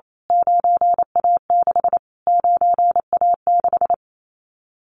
Morse code, heard as a keyed tone, about 28 wpm, 700 Hz. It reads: 9A6 9A6
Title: Callsign Prefixes for Top 25 DXCC - Repeated 3x - 28wpm